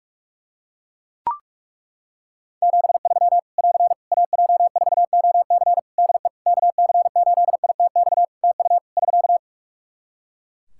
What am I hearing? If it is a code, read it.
Morse code, heard as a keyed tone, about 41 wpm, 700 Hz. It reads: 73 <AR> AJ4OC DE KC8ITX TU <SK>
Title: Practice QSO - Exchange 4 - Ending - 45wpm